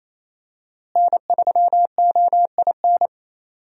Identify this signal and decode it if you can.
Morse code, heard as a keyed tone, about 28 wpm, 700 Hz. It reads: N3OIN
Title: USA Call Signs - 5 Characters - Repeated 3x - 28wpm